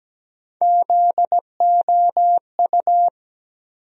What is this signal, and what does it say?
Morse code, heard as a keyed tone, about 17 wpm, 700 Hz. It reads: ZOU
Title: USA Call Sign Postfixes - 3 Letters 17wpm